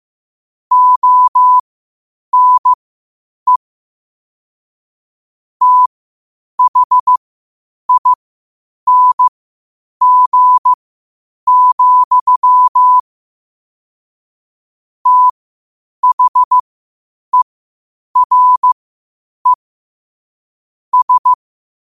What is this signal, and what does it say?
Morse code, heard as a keyed tone, about 15 wpm, spaced as for 8 wpm, 1000 Hz. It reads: ONE THING, THERE S
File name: qotdfiles / 8wpm